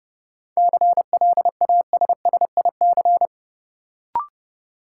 Morse code, heard as a keyed tone, about 30 wpm, 700 Hz. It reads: CLASSIC
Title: Sentences from Top 1000 Words - New Word Drill - 30wpm